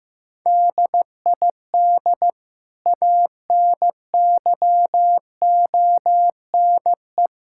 Morse code, heard as a keyed tone, about 16 wpm, 700 Hz. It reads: DID ANYONE
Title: Sentences from Top 1600 Words 15wpm